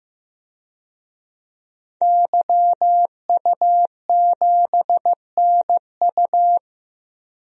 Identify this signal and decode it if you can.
Morse code, heard as a keyed tone, about 15 wpm, 700 Hz. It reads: YU7NU
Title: DXCC-100 Call Signs - 5 Characters - 15wpm